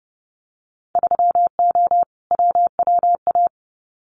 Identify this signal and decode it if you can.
Morse code, heard as a keyed tone, about 30 wpm, 700 Hz. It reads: 3O WWA
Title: Winter Field Day Exchanges Encore 30wpm